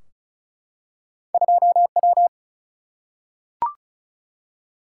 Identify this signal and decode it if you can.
Morse code, heard as a keyed tone, about 36 wpm, 700 Hz. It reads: 2W
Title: Single-Letter Single-Number 35wpm